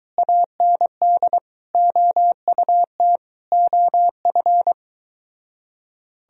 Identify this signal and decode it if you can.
Morse code, heard as a keyed tone, about 23 wpm, 700 Hz. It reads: AND OUT OF